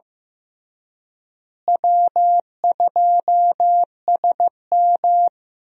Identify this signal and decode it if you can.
Morse code, heard as a keyed tone, about 15 wpm, 700 Hz. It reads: W2SM
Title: USA Call Signs - 4 Characters - Repeated 3x - 15wpm